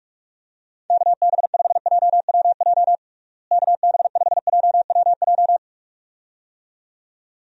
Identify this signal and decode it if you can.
Morse code, heard as a keyed tone, about 46 wpm, 700 Hz. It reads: KB5JWJ KB5JWJ
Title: USA Call Signs - 6 Characters - Repeated 2x - 45wpm